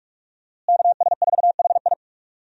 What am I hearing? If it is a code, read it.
Morse code, heard as a keyed tone, about 41 wpm, 700 Hz. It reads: KS4HI
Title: USA Call Signs - 5 Characters - Repeated 3x - 45wpm